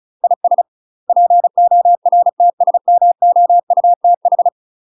Morse code, heard as a keyed tone, about 36 wpm, 700 Hz. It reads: IS PORTSMOUTH
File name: UK MASTER QSOs TRAINING/35WPM